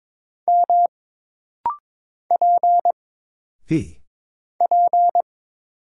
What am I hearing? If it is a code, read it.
Morse code, heard as a keyed tone, about 22 wpm, 700 Hz. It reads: M P P
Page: This rapid-fire timing is geared to push your ICR (Instant Character Recognition) skills.